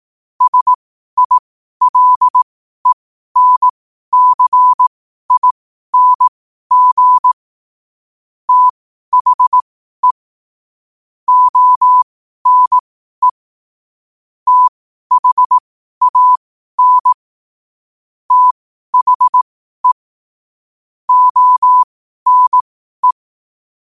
Morse code, heard as a keyed tone, about 18 wpm, spaced as for 12 wpm, 1000 Hz. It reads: SILENCING THE ONE THAN THE ONE
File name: qotdfiles / 12wpm